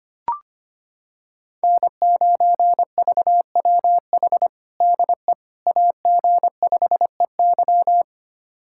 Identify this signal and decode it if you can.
Morse code, heard as a keyed tone, about 25 wpm, 700 Hz. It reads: N9VWH DE AG5EY
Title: USA Callsign DE Callsign - 5 Characters - 25wpm